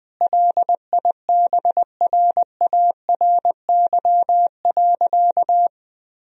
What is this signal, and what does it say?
Morse code, heard as a keyed tone, about 20 wpm, 700 Hz. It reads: LIBRARY.
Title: Sentences from Top 1000 Words - Review Part 4 - 20wpm